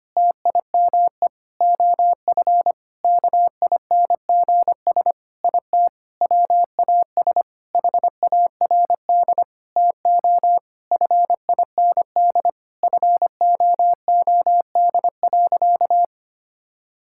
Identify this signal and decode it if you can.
Morse code, heard as a keyed tone, about 25 wpm, 700 Hz. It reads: TIME OF KINGS IT WAS HARD TO FIND FOOD.
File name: practice / 25wpm / Sentences with 3-Word Phrases - 25wpm